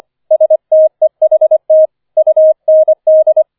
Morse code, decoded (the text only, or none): STEHT UND